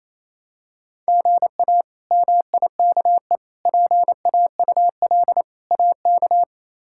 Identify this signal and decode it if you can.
Morse code, heard as a keyed tone, about 28 wpm, 700 Hz. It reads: GA MIKE PAUL AK
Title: K1USN SST QSO Practice - 28wpm